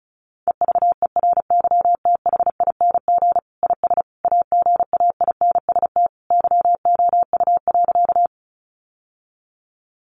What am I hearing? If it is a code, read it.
Morse code, heard as a keyed tone, about 34 wpm, 700 Hz. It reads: EVERYTHING IS AGAINST YOU.
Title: Sentences from Top 700 Words 35wpm